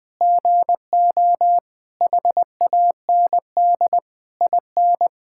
Morse code, heard as a keyed tone, about 20 wpm, 700 Hz. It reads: GO HAND IN
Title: Sentences from Top 1000 Words - Review Part 6 - 20wpm